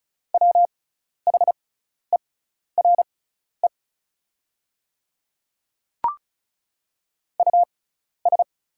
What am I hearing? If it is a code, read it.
Morse code, heard as a keyed tone, about 36 wpm, spaced as for 12 wpm, 690 Hz. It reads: WHERE US
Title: 6x Character Spacing - Top 200 Words - 35wpm